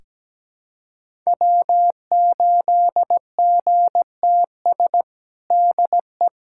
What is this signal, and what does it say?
Morse code, heard as a keyed tone, about 17 wpm, 700 Hz. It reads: W8GTS DE
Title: USA Callsign DE Callsign - 5 Characters - 17wpm